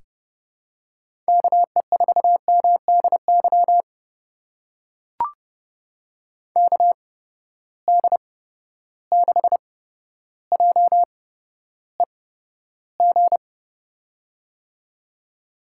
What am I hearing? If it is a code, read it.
Morse code, heard as a keyed tone, about 30 wpm, spaced as for 13 wpm, 700 Hz. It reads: KE4MDY KD6JEG
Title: USA Call Signs - 6 Characters - 8x Character Spacing - 30wpm